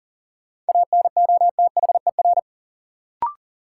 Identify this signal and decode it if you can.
Morse code, heard as a keyed tone, about 40 wpm, 700 Hz. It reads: ANOTHER
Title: Sets of 5 Words Encore - New Word Drill - 40wpm